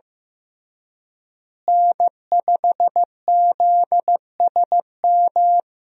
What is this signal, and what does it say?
Morse code, heard as a keyed tone, about 15 wpm, 700 Hz. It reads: N5ZSM
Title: USA Call Signs - 5 Characters - Repeated 3x - 15wpm